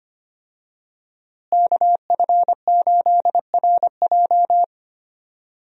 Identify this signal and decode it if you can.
Morse code, heard as a keyed tone, about 25 wpm, 700 Hz. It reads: KF8RJ
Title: USA Call Signs - 5 Characters - Repeated 3x - 25wpm